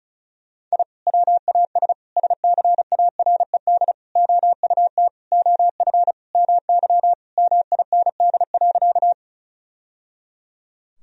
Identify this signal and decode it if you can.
Morse code, heard as a keyed tone, about 34 wpm, 700 Hz. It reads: I WAS SCARED OUT OF MY MIND.
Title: Sentences from Top 1000 Words - Review Part 4 - 35wpm